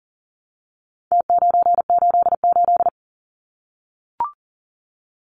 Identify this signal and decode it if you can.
Morse code, heard as a keyed tone, about 40 wpm, 700 Hz. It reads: T988
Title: Numbers - Contest Style - 4-Digit 40wpm